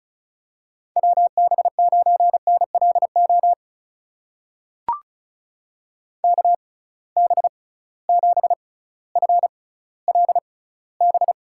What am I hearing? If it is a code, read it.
Morse code, heard as a keyed tone, about 34 wpm, spaced as for 18 wpm, 700 Hz. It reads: WB9NLO KB7FLB
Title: USA Call Signs - 6 Characters - 6x Character Spacing - 35wpm